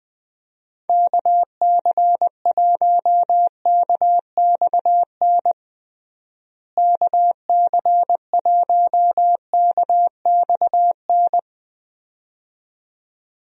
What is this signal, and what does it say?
Morse code, heard as a keyed tone, about 20 wpm, 700 Hz. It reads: KC1KXN KC1KXN
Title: USA Call Signs - 6 Characters - Repeated 2x - 20wpm